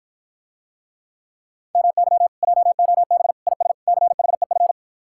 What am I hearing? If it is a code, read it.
Morse code, heard as a keyed tone, about 54 wpm, 700 Hz. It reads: MY JOB IS CHEF